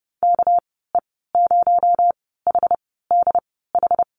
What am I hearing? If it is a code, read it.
Morse code, heard as a keyed tone, about 30 wpm, spaced as for 17 wpm, 700 Hz. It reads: KE0HDH
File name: Lesson 37g - Letter x - Call Signs 30wpm